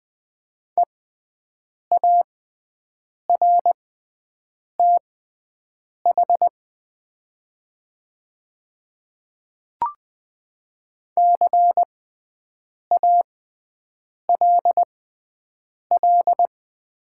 Morse code, heard as a keyed tone, about 21 wpm, spaced as for 7 wpm, 700 Hz. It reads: EARTH CALL
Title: 6x Character Spacing - Top 200 Words - 20wpm